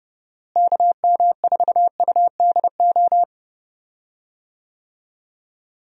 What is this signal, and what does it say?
Morse code, heard as a keyed tone, about 28 wpm, 700 Hz. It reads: KM4UDO
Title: USA Call Signs - 6 Characters - Repeated 2x - 30wpm